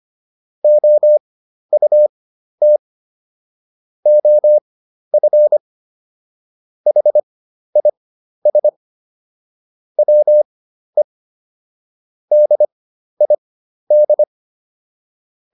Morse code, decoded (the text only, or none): OUT OF HIS WE DID